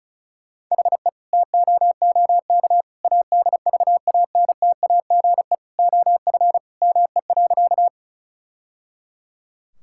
Morse code, heard as a keyed tone, about 36 wpm, 700 Hz. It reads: HE TOOK ADVANTAGE OF ME.
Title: Sentences from Top 1000 Words - Review Part 6 - 35wpm